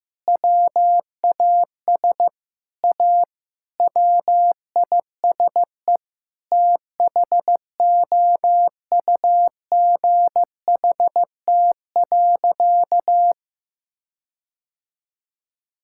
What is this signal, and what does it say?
Morse code, heard as a keyed tone, about 15 wpm, 700 Hz. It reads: WAS A WISE THOUGHT.
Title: Sentences from Top 1600 Words 15wpm